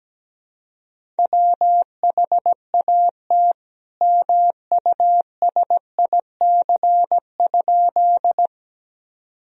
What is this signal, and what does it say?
Morse code, heard as a keyed tone, about 17 wpm, 700 Hz. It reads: WHAT MUSIC?
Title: Sentences from Top 1000 Words - Review Part 1 - 17wpm